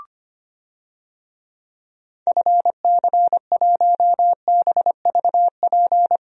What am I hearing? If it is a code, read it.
Morse code, heard as a keyed tone, about 25 wpm, 700 Hz. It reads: FC1BVP
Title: DXCC-100 Call Signs - 6 Characters - Repeated 3x - 25wpm